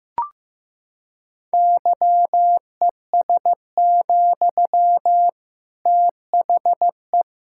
Morse code, heard as a keyed tone, about 15 wpm, 700 Hz. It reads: YES, THE
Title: Sentences from Top 1600 Words 15wpm